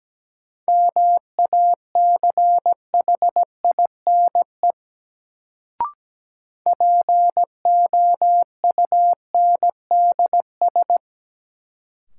Morse code, heard as a keyed tone, about 17 wpm, 700 Hz. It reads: MACHINE POUNDS
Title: Sentences from Top 1000 Words - Review Part 2 - New Word Drill - 17wpm